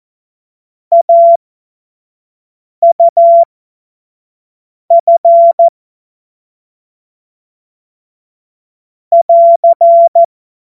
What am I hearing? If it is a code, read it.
Morse code, heard as a keyed tone, about 14 wpm, spaced as for 5 wpm, 680 Hz. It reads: AUF <AR>
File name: Morsekurs / Witze / Witz 10.Oktober 2023 25 F70